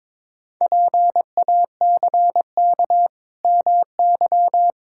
Morse code, heard as a keyed tone, about 22 wpm, 700 Hz. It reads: PACK MY
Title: Pangram Challenge v2 - 22wpm